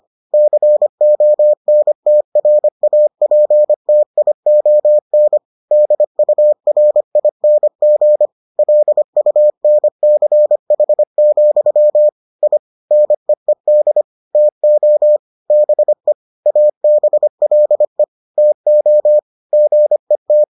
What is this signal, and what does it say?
Morse code, heard as a keyed tone, about 25 wpm, 600 Hz. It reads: CONTRAPTION DURING LUNCH, I NEED TO BE ABLE TO GET